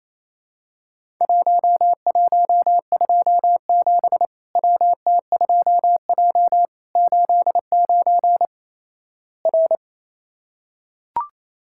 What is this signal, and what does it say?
Morse code, heard as a keyed tone, about 28 wpm, 700 Hz. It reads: 1127 WT2J 89 R
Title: Worked All Europe DX Contest - QTC Exchange - 28wpm